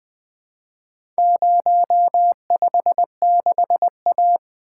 Morse code, heard as a keyed tone, about 20 wpm, 700 Hz. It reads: 056A
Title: Four Letter-Number Combinations - 20wpm